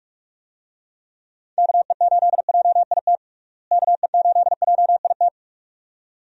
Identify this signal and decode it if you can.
Morse code, heard as a keyed tone, about 46 wpm, 700 Hz. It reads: KE8JIT KE8JIT
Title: USA Call Signs - 6 Characters - Repeated 2x - 45wpm